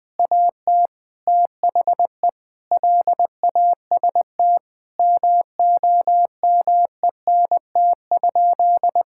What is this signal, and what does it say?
Morse code, heard as a keyed tone, about 20 wpm, 700 Hz. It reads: AT THE LAST MOMENT?
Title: Sentences from Top 1600 Words 20wpm